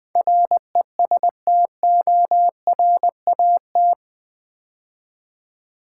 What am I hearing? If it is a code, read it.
Morse code, heard as a keyed tone, about 20 wpm, 700 Hz. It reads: RESTORAT
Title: Build A Word - 4 Letters At A Time - 11-Letter Words - 20wpm